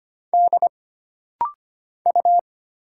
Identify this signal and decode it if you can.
Morse code, heard as a keyed tone, about 25 wpm, 700 Hz. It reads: D U
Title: Single Letter-Number Alt - Rapid-Fire 25wpm